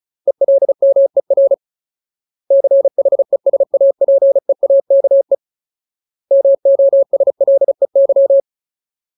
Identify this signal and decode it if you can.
Morse code, heard as a keyed tone, about 34 wpm, 550 Hz. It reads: ELMER CHESAPEAKE MOSLEY
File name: Ultimate Word List / 35WPM Ultimate